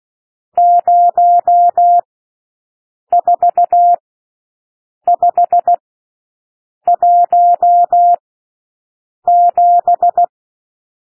Morse code, heard as a keyed tone, about 16 wpm, spaced as for 6 wpm, 700 Hz. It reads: 04517
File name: SOUND NUMBERS 16 wpm